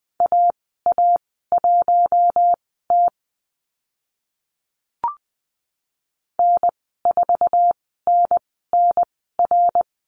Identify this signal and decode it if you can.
Morse code, heard as a keyed tone, about 20 wpm, spaced as for 14 wpm, 700 Hz. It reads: AA1T N4NNR
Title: 062 Lesson 11g - Letter h - Call Signs 20wpm